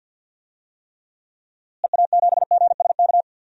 Morse code, heard as a keyed tone, about 46 wpm, 700 Hz. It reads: EA7GSK
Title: DXCC-100 Call Signs - 6 Characters - Repeated 3x - 50wpm